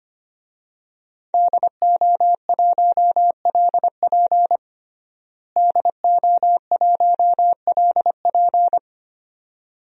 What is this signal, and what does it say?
Morse code, heard as a keyed tone, about 25 wpm, 700 Hz. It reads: DO1LP DO1LP
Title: DXCC-100 Call Signs - 5 Characters - Repeated 2x - 25wpm